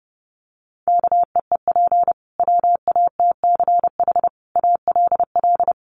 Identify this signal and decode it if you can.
Morse code, heard as a keyed tone, about 30 wpm, 700 Hz. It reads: KEEP WATCH ALL
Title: Sentences from Top 300 Words 30wpm